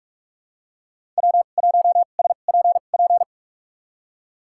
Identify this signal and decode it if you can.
Morse code, heard as a keyed tone, about 46 wpm, spaced as for 33 wpm, 700 Hz. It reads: W1SPP
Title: USA Call Signs - 5 Characters - 2x Character Spacing - 45wpm